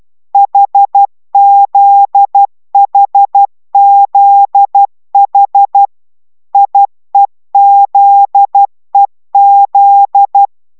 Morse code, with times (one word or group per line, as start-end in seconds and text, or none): 0.3-5.9 HZHZH
6.5-10.5 IEZEZ